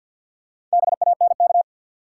Morse code, heard as a keyed tone, about 52 wpm, 700 Hz. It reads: BANK
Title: Four Letter Words - 50wpm